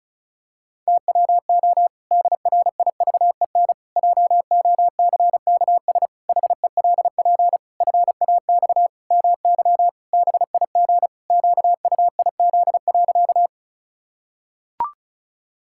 Morse code, heard as a keyed tone, about 36 wpm, 700 Hz. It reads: TWO DRIVEN JOCKS HELP FAX MY BIG QUIZ.
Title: Pangram Challenge v2 - 35wpm